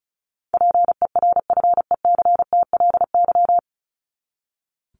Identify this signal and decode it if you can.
Morse code, heard as a keyed tone, about 34 wpm, 700 Hz. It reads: PERFECTLY
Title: Top 1501-1750 Words 35wpm